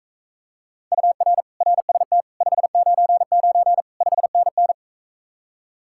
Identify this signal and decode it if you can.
Morse code, heard as a keyed tone, about 42 wpm, 700 Hz. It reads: UR RST 599 5NN